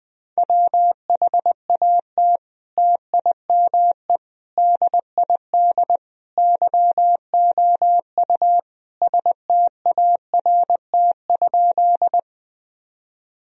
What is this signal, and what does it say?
Morse code, heard as a keyed tone, about 21 wpm, 700 Hz. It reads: WHAT TIME DID YOU START?
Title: Sentences from Top 1000 Words - Review Part 1 - 20wpm